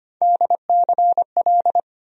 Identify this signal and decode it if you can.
Morse code, heard as a keyed tone, about 25 wpm, 700 Hz. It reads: DCL
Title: USA Call Sign Postfixes - 3 Letters 25wpm